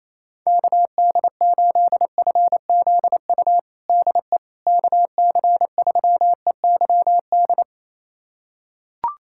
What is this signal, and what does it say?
Morse code, heard as a keyed tone, about 28 wpm, 700 Hz. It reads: KD8FZU DE KC3EYD
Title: USA Callsign DE Callsign 28wpm